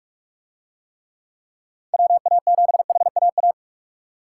Morse code, heard as a keyed tone, about 44 wpm, 700 Hz. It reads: WA7HAA